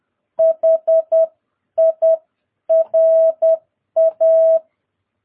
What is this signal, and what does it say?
Morse code, heard as a keyed tone, about 10 wpm, spaced as for 9 wpm, 650 Hz. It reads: HIRA
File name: jh8cbh / 2022cw